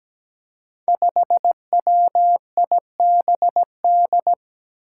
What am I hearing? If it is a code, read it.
Morse code, heard as a keyed tone, about 17 wpm, 700 Hz. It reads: 5WIBD
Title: Five Challenging Letter-Number Combinations 17wpm